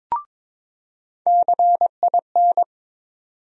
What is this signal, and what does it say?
Morse code, heard as a keyed tone, about 21 wpm, 700 Hz. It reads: CIN
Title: USA Call Sign Postfixes - 3 Letters 22wpm